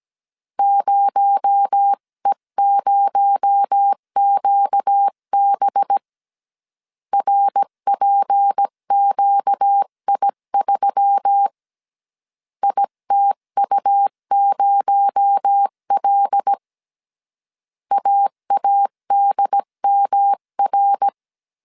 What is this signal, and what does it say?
Morse code, heard as a keyed tone, about 17 wpm, 800 Hz. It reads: 0E0QB RPQI3 ITU0L AADMR
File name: CW / DET / 1 6 1500C S C 16wpm 2733